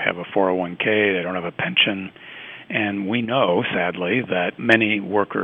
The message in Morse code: none

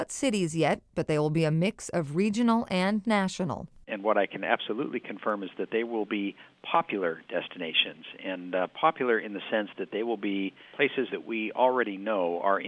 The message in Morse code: none

